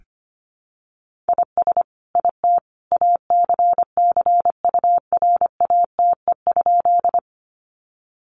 Morse code, heard as a keyed tone, about 25 wpm, 700 Hz. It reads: IS IT ACCURATE?
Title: Sentences from Top 1200 Words 25wpm